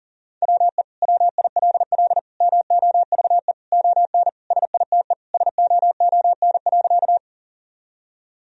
Morse code, heard as a keyed tone, about 37 wpm, 700 Hz. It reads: WE WILL MOVE ON SITE SOON.
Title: Sentences from Top 1000 Words - Review Part 4 - 40wpm